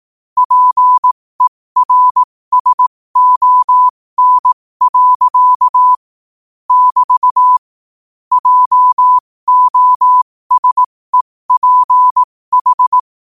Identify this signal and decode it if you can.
Morse code, heard as a keyed tone, about 18 wpm, spaced as for 15 wpm, 1000 Hz. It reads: PERSON. <BT> JOSEPH
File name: qotdfiles / 15wpm